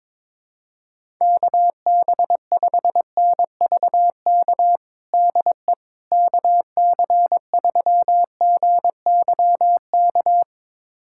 Morse code, heard as a keyed tone, about 22 wpm, 700 Hz. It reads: KB5NVK DE KC3GYK
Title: USA Callsign DE Callsign 22wpm